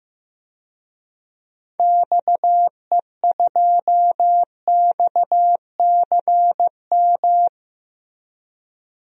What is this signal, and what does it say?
Morse code, heard as a keyed tone, about 15 wpm, 700 Hz. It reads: XE2XCM
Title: DXCC-100 Call Signs - 6 Characters - Repeated 3x - 15wpm